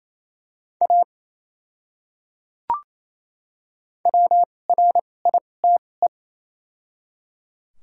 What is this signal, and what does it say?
Morse code, heard as a keyed tone, about 27 wpm, spaced as for 20 wpm, 700 Hz. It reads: A WRITE
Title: 2x Character Spacing - Top 100 Words - 28wpm